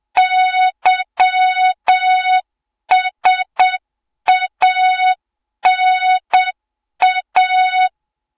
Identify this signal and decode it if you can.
Morse code, heard as a keyed tone, about 7 wpm, 3700 Hz, with noise. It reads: YSANA
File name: Morze-HA2MN / hang / 20 GYAKORLAT